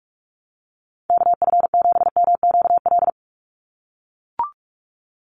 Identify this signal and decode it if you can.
Morse code, heard as a keyed tone, about 44 wpm, 700 Hz. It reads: KF7MQL
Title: USA Call Signs 45wpm